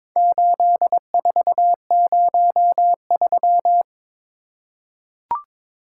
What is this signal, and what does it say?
Morse code, heard as a keyed tone, about 22 wpm, 700 Hz. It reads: 8403
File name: practice / 22wpm / Four Letter-Number Combinations - 22wpm